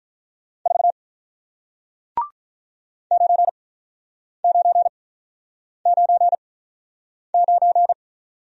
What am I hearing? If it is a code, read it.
Morse code, heard as a keyed tone, about 46 wpm, spaced as for 9 wpm, 700 Hz. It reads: 4 9999